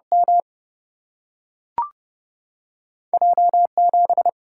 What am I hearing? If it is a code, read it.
Morse code, heard as a keyed tone, about 30 wpm, 700 Hz. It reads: M J 7
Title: Single-Letter Single-Number 30wpm